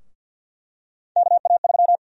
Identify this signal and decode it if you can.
Morse code, heard as a keyed tone, about 48 wpm, 700 Hz. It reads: KA3
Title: USA Call Sign Prefixes - Repeated 2x - 50wpm